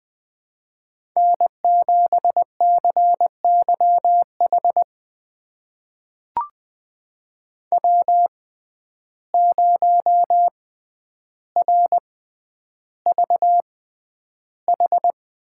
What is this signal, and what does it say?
Morse code, heard as a keyed tone, about 21 wpm, spaced as for 10 wpm, 700 Hz. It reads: N7CYH W0RVH
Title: USA Call Signs - 5 Characters - 6x Character Spacing - 20wpm